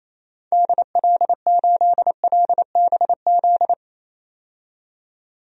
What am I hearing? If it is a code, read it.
Morse code, heard as a keyed tone, about 28 wpm, 700 Hz. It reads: DL8LBZ
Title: DXCC-100 Call Signs - 6 Characters - 28wpm